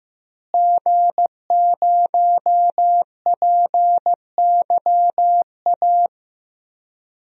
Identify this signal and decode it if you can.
Morse code, heard as a keyed tone, about 15 wpm, 700 Hz. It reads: G0PYA
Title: DXCC-100 Call Signs - 5 Characters - Repeated 3x - 15wpm